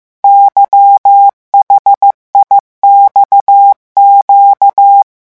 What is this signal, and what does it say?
Morse code, heard as a keyed tone, about 15 wpm, 800 Hz. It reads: YHIXQ
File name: morse mp3 / revision8-vitesse 14 mots